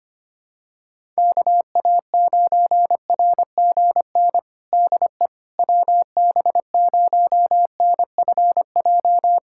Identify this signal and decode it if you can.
Morse code, heard as a keyed tone, about 25 wpm, 700 Hz. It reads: KA9RGN DE WB0NFJ
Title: USA Callsign DE Callsign - 6 Characters - 25wpm